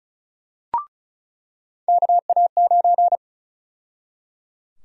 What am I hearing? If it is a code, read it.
Morse code, heard as a keyed tone, about 36 wpm, 700 Hz. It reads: KA9
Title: USA Call Sign Prefixes - 35wpm